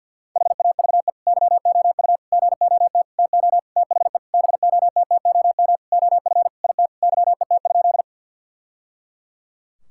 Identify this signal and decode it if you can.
Morse code, heard as a keyed tone, about 52 wpm, 700 Hz. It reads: HAVE YOU GOT TO THE BOTTOM OF IT YET?
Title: Sentences from Top 1000 Words - Review Part 6 - 50wpm